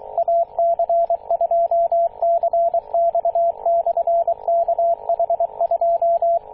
Morse code, heard as a keyed tone, about 23 wpm, 680 Hz, with radio noise. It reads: AC2CX/KH2
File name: posts / 2016-june-26-0520z-ac2cx-kh2-21-027mhz-cw